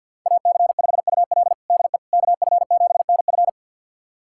Morse code, heard as a keyed tone, about 51 wpm, 700 Hz. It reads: AC5UL DE KF7NF
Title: USA Callsign DE Callsign 50wpm